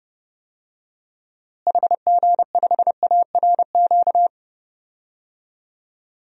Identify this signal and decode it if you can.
Morse code, heard as a keyed tone, about 32 wpm, 700 Hz. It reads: HG5ARQ
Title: DXCC-100 Call Signs - 6 Characters - Repeated 2x - 30wpm